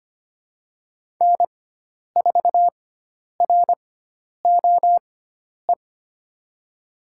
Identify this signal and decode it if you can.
Morse code, heard as a keyed tone, about 25 wpm, spaced as for 10 wpm, 700 Hz. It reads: N4ROE
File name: Lesson 16g - Letter c - Call Signs 25wpm